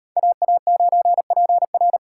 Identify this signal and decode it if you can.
Morse code, heard as a keyed tone, about 36 wpm, 700 Hz. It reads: AA9PR